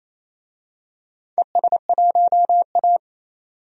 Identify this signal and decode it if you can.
Morse code, heard as a keyed tone, about 27 wpm, 690 Hz. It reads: ES1A
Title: DXCC-100 Call Signs - 4 Characters - 28wpm